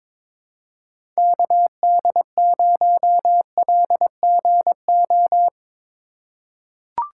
Morse code, heard as a keyed tone, about 22 wpm, 700 Hz. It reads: KD0LGO
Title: USA Call Signs - 4x Character Spacing - 22wpm